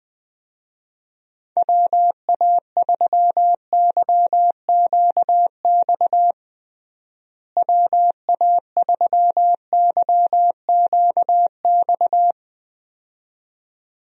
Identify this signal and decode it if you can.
Morse code, heard as a keyed tone, about 20 wpm, 700 Hz. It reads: WA3YQX WA3YQX
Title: USA Call Signs - 6 Characters - Repeated 2x - 20wpm